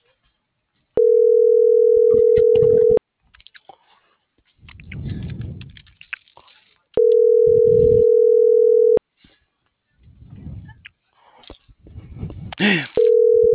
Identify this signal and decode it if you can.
Ringback tone (North America), 440 + 480 Hz.